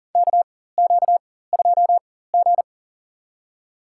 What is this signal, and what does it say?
Morse code, heard as a keyed tone, about 40 wpm, spaced as for 19 wpm, 700 Hz. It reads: KQ2G
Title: USA Call Signs - 4 Characters - 4x Character Spacing - 40wpm